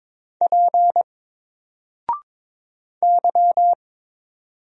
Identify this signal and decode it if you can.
Morse code, heard as a keyed tone, about 22 wpm, 700 Hz. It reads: P Y
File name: Single-Letter-Number-22wpm